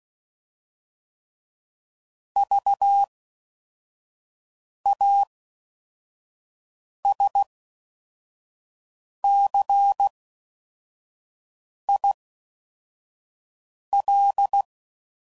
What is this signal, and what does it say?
Morse code, heard as a keyed tone, about 16 wpm, 800 Hz. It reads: V A S C I L